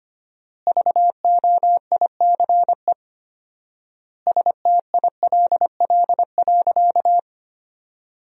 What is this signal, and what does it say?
Morse code, heard as a keyed tone, about 25 wpm, 700 Hz. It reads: VOICE STILL.
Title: 4x Word-Spacing - Sentences from Top 500 Words - 25wpm